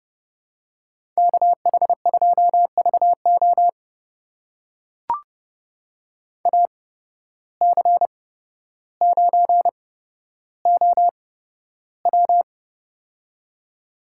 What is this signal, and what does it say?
Morse code, heard as a keyed tone, about 32 wpm, spaced as for 13 wpm, 700 Hz. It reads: KH2VO AC9OW
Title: USA Call Signs - 5 Characters - 8x Character Spacing - 30wpm